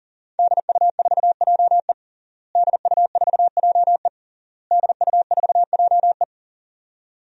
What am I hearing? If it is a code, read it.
Morse code, heard as a keyed tone, about 40 wpm, 700 Hz. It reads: DU4JE DU4JE DU4JE
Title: DXCC-100 Call Signs - 5 Characters - Repeated 3x - 40wpm